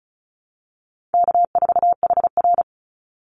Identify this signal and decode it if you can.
Morse code, heard as a keyed tone, about 34 wpm, 700 Hz. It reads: K4HR
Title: Call Signs Repeated 2x - 35wpm